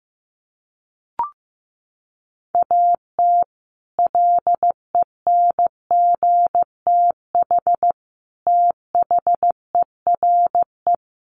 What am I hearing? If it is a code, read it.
Morse code, heard as a keyed tone, about 15 wpm, 700 Hz. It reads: AT LENGTH THERE
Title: Sentences from Top 1000 Words 15wpm